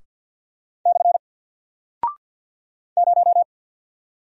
Morse code, heard as a keyed tone, about 48 wpm, 700 Hz. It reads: / 0
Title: Single Character (Letter:Number:Punctuation:Prosign) Alt 50wpm